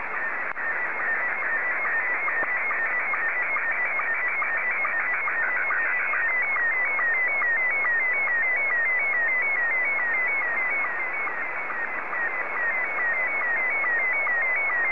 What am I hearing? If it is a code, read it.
SSTV (slow-scan television), Scottie 1.